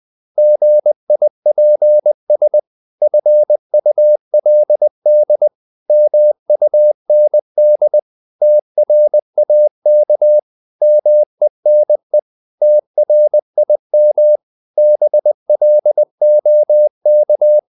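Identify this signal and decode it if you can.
Morse code, heard as a keyed tone, about 20 wpm, 600 Hz. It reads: GIPS FULD MUND TRAK MENE TRIM BLOK
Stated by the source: Morse code practice of short Danish words at 20 wpm, from CW med Gnister.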